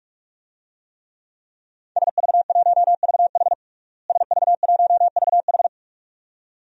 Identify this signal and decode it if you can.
Morse code, heard as a keyed tone, about 46 wpm, 700 Hz. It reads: SV1VH SV1VH
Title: DXCC-100 Call Signs - 5 Characters - Repeated 2x - 45wpm